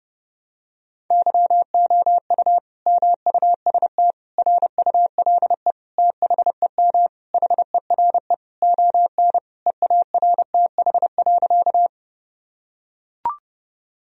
Morse code, heard as a keyed tone, about 32 wpm, 700 Hz. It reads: YOU MUST RULE THEM HERE ON EARTH.
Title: Sentences from Top 1000 Words - Review Part 2 - 30wpm